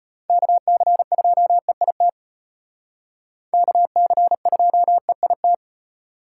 Morse code, heard as a keyed tone, about 32 wpm, 700 Hz. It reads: KC2EIT KC2EIT